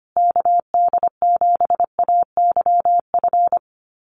Morse code, heard as a keyed tone, about 25 wpm, 700 Hz. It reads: KD7AYF
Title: USA Call Signs Repeated 3x 25wpm